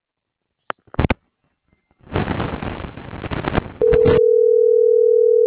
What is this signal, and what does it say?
Ringback tone (North America), 440 + 480 Hz.
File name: CENTRAL TERRESTRE / out vane.lazaro - .41393